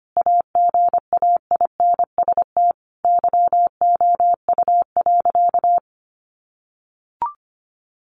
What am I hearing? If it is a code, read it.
Morse code, heard as a keyed tone, about 25 wpm, 700 Hz. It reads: AGAINST YOU.
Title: Sentences from Top 1400 Words 25wpm